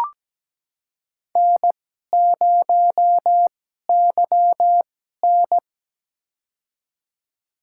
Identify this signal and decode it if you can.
Morse code, heard as a keyed tone, about 17 wpm, spaced as for 12 wpm, 700 Hz. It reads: N0YN
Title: USA Call Signs - 4 Characters - 2x Character Spacing - 17wpm